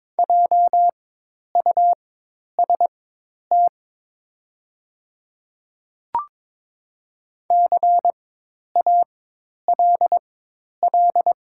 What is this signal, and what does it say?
Morse code, heard as a keyed tone, about 22 wpm, spaced as for 10 wpm, 700 Hz. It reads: JUST CALL
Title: 4x Character Spacing - Top 300 Words - 22wpm